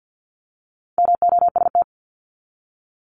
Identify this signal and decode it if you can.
Morse code, heard as a keyed tone, about 50 wpm, 700 Hz. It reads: MOST
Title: Top 500 Words 50wpm